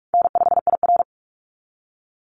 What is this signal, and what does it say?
Morse code, heard as a keyed tone, about 44 wpm, 700 Hz. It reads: N5IR
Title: Call Signs Repeated 3x - 45wpm